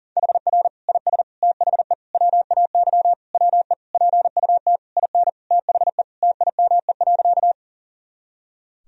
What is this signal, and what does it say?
Morse code, heard as a keyed tone, about 40 wpm, 700 Hz. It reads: HR IS THE WAY WE PUT IN THE TIME.
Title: Sentences from Top 200 Words and Abbreviations 40wpm